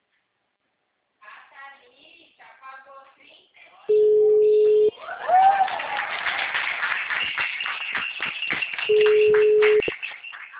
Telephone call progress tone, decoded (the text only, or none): ringback tone